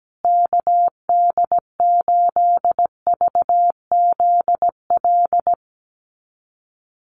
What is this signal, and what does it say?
Morse code, heard as a keyed tone, about 17 wpm, 700 Hz. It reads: KD8VZL
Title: USA Call Signs Repeated 3x 17wpm